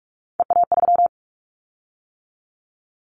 Morse code, heard as a keyed tone, about 46 wpm, 700 Hz. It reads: EA3
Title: Callsign Prefixes for Top 25 DXCC - 45wpm